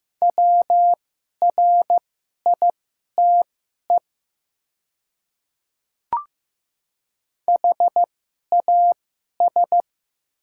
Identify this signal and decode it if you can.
Morse code, heard as a keyed tone, about 15 wpm, spaced as for 11 wpm, 700 Hz. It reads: WRITE HAS
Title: 2x Character Spacing - Top 300 Words - 15wpm